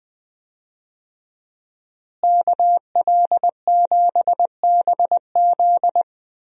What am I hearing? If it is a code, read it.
Morse code, heard as a keyed tone, about 20 wpm, 700 Hz. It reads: KL7BZ
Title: DXCC-100 Call Signs - 5 Characters - Repeated 2x - 20wpm